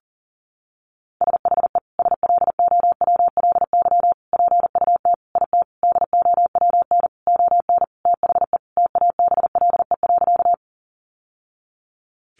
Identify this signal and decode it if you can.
Morse code, heard as a keyed tone, about 40 wpm, 700 Hz. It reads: SHE SLOWLY PUT IT DOWN ON THE TABLE.
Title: Sentences from Top 1100 Words 40wpm